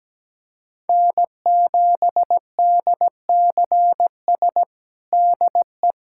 Morse code, heard as a keyed tone, about 17 wpm, 700 Hz. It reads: N7DCS DE
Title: USA Callsign DE Callsign 17wpm